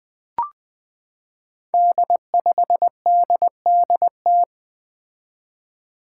Morse code, heard as a keyed tone, about 20 wpm, 700 Hz. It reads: D5DDT
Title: Five Challenging Letter-Number Combinations 20wpm